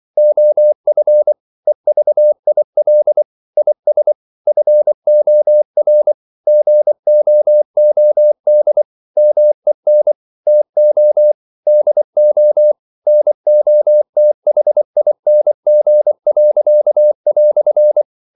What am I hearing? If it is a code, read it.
Morse code, heard as a keyed tone, about 24 wpm, 600 Hz. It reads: OF EVIL IS FOR GOOD MEN TO DO NOTHING."
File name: CW / Practice / EnglishQuotations0424